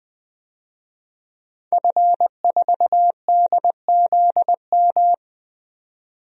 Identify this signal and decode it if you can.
Morse code, heard as a keyed tone, about 20 wpm, 700 Hz. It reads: F4DZM
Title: DXCC-100 Call Signs - 5 Characters - 20wpm